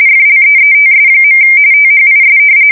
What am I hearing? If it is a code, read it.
RTTY, about 45.45 baud. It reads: FLEETS. THIS
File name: fsk170 / battleshipnj